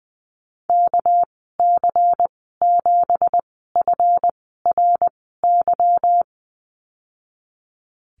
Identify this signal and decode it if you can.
Morse code, heard as a keyed tone, about 20 wpm, spaced as for 15 wpm, 700 Hz. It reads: KC7FRY
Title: 200 Lesson 32g - Letter k - Call Signs 20wpm